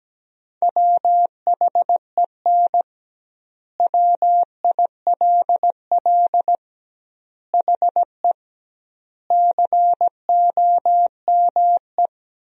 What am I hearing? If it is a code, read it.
Morse code, heard as a keyed tone, about 17 wpm, 700 Hz. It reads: WHEN WILL HE COME
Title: 2x Word-Spacing - Sentences from Top 200 Words - 17wpm